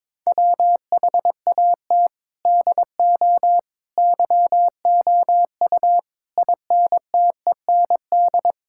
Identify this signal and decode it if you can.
Morse code, heard as a keyed tone, about 22 wpm, 700 Hz. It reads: WHAT DO YOU INTEND
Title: Sentences from Top 1600 Words 22wpm